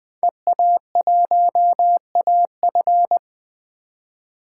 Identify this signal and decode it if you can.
Morse code, heard as a keyed tone, about 21 wpm, 700 Hz. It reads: EA1AF
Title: 2023 POTA Hunter Call Signs - 20wpm